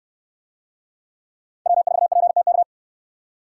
Morse code, heard as a keyed tone, about 64 wpm, 700 Hz. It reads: W4PTK